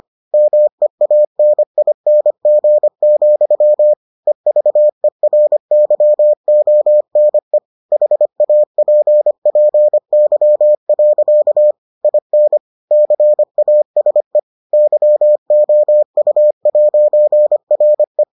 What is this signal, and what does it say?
Morse code, heard as a keyed tone, about 25 wpm, 600 Hz. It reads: MEANING, EVERYONE HAPPY. IN CASE YOU'RE